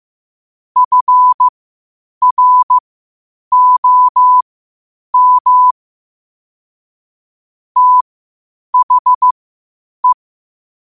Morse code, heard as a keyed tone, about 15 wpm, spaced as for 8 wpm, 1000 Hz. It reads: FROM THE
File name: qotdfiles / 8wpm